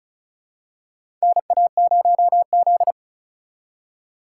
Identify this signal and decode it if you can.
Morse code, heard as a keyed tone, about 34 wpm, 700 Hz. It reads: NA0Z
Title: USA Call Signs - 4 Characters - 2x Character Spacing - 35wpm